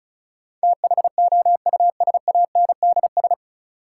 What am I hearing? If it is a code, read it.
Morse code, heard as a keyed tone, about 36 wpm, 700 Hz. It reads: THOUSANDS
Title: Sets of 4 Words - New Word Drill - 35wpm